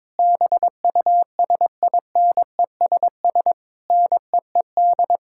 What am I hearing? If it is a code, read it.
Morse code, heard as a keyed tone, about 22 wpm, 700 Hz. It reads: BUSINESS NEED
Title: Sentences from Top 1000 Words - Review Part 6 - 22wpm